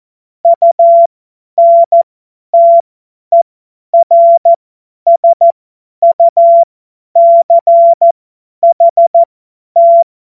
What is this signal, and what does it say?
Morse code, heard as a keyed tone, about 14 wpm, spaced as for 10 wpm, 680 Hz. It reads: UNTERSUCHT
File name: Morsekurs / Witze / Witz 12 März 2024 50 F70